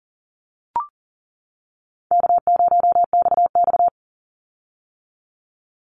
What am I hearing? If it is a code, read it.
Morse code, heard as a keyed tone, about 42 wpm, 700 Hz. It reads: K0XX
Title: Easy Call Signs - 40wpm